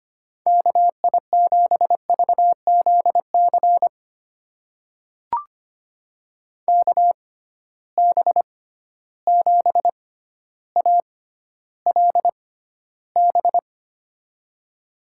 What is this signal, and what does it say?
Morse code, heard as a keyed tone, about 25 wpm, spaced as for 13 wpm, 700 Hz. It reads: KI7VZC KB7ALB
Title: USA Call Signs - 6 Characters - 6x Character Spacing - 25wpm